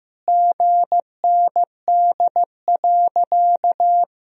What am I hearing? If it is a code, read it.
Morse code, heard as a keyed tone, about 15 wpm, 700 Hz. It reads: GND.
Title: Sentences from Top 500 Words and Abbreviations 15wpm